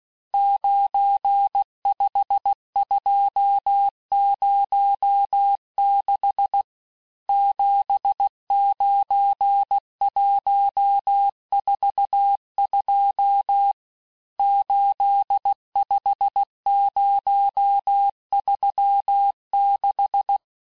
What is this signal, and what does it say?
Morse code, heard as a keyed tone, about 16 wpm, 800 Hz. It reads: 95206 79142 85036
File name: cours-du-lundi files / cours-1592--teste-chiffre-vitesse-15mpm